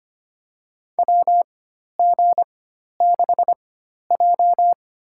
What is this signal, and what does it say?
Morse code, heard as a keyed tone, about 25 wpm, spaced as for 12 wpm, 690 Hz. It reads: WG6J
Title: USA Call Signs - 4 Characters - 4x Character Spacing - 25wpm